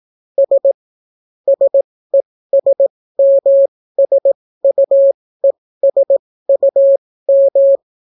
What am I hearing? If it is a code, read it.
Morse code, heard as a keyed tone, about 18 wpm, spaced as for 14 wpm, 550 Hz. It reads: S SESMSUESUM